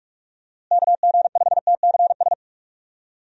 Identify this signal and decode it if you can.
Morse code, heard as a keyed tone, about 46 wpm, 700 Hz. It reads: KG5TCS
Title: USA Call Signs - 6 Characters - Repeated 2x - 45wpm